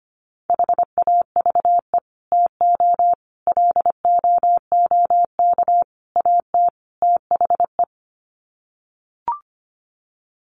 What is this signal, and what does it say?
Morse code, heard as a keyed tone, about 25 wpm, 700 Hz. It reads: HAVE TO LOOK AT THE
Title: Sets of 5 Words Encore 25wpm